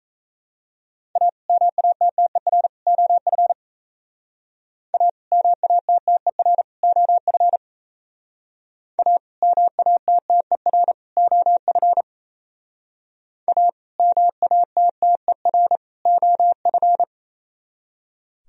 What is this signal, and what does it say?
Morse code, heard as a keyed tone, about 35 wpm, 700 Hz. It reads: A MATTER OF A MATTER OF A MATTER OF A MATTER OF
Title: Speed-Racing - Sets of 3 Words 28wpm